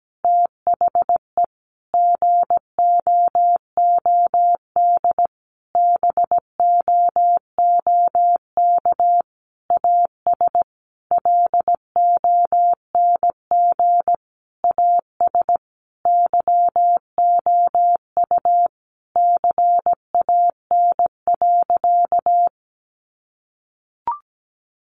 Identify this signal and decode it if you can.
Morse code, heard as a keyed tone, about 17 wpm, 700 Hz. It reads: THE GOOD BOOK AS LONG AS YOU CAN.
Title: Sentences from Top 1200 Words 17wpm